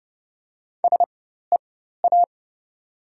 Morse code, heard as a keyed tone, about 30 wpm, spaced as for 14 wpm, 700 Hz. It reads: SEA
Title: 4x Character Spacing - Top 300 Words - 30wpm